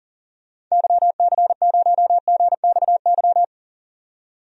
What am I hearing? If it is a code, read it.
Morse code, heard as a keyed tone, about 40 wpm, 700 Hz. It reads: YC0GXY
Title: DXCC-100 Call Signs - 6 Characters - 40wpm